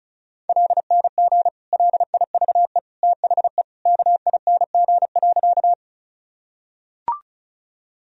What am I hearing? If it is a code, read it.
Morse code, heard as a keyed tone, about 36 wpm, 700 Hz. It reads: LNG LIVE THE KING.
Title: Sentences from Top 400 Words and Abbreviations 35wpm